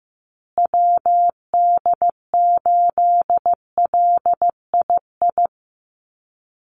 Morse code, heard as a keyed tone, about 15 wpm, 700 Hz. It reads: WD8LII
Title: USA Call Signs Repeated 2x 15wpm